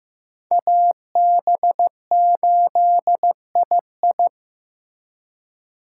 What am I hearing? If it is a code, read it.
Morse code, heard as a keyed tone, about 18 wpm, 700 Hz. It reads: AB8II
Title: Speed Racing - USA 2x2 Call Signs 15wpm